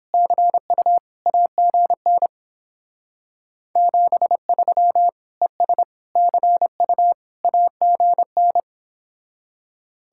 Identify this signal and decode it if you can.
Morse code, heard as a keyed tone, about 29 wpm, 700 Hz. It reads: CU AGN 73 ES CU AGN